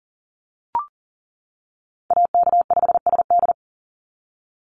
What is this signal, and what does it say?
Morse code, heard as a keyed tone, about 40 wpm, 700 Hz. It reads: AK5SD
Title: NA SOTA Chaser Callsigns 40wpm